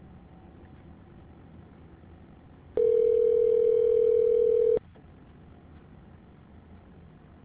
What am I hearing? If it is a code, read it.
Ringback tone (North America), 440 + 480 Hz.